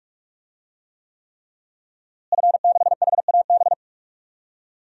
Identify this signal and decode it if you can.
Morse code, heard as a keyed tone, about 46 wpm, 700 Hz. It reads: F6HAB
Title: DXCC-100 Call Signs - 5 Characters - Repeated 3x - 45wpm